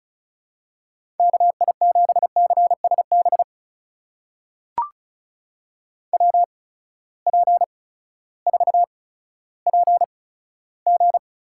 Morse code, heard as a keyed tone, about 34 wpm, spaced as for 16 wpm, 700 Hz. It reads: KI7CSB WP4PG
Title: USA Call Signs - 6 Characters - 8x Character Spacing - 35wpm